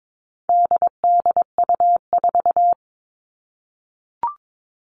Morse code, heard as a keyed tone, about 22 wpm, 700 Hz. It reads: DDU4
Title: All Cut Numbers - 4-Digit 22wpm